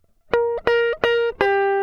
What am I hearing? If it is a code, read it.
There is no code.